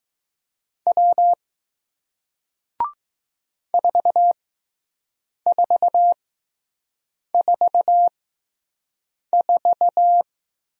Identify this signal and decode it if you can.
Morse code, heard as a keyed tone, about 20 wpm, spaced as for 6 wpm, 700 Hz. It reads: W 4444